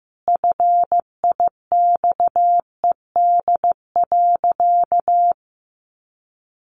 Morse code, heard as a keyed tone, about 15 wpm, 700 Hz. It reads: FIXED.
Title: Sentences from Top 1100 Words 15wpm